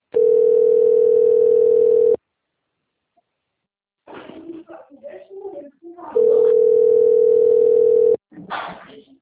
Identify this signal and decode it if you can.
Ringback tone (North America), 440 + 480 Hz, with speech.
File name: rg-1600-unknown- - - .8637